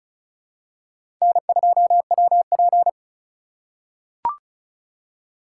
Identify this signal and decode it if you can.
Morse code, heard as a keyed tone, about 36 wpm, 700 Hz. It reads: N2WP
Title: USA Call Signs - 2x Character Spacing - 35wpm